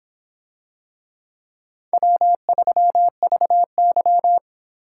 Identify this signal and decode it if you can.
Morse code, heard as a keyed tone, about 25 wpm, 700 Hz. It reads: W3VY